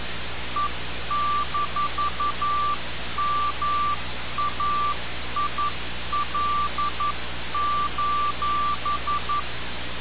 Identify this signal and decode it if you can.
Morse code, heard as a keyed tone, about 11 wpm, 1170 Hz. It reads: E-MAIL:
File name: radiocap / REC z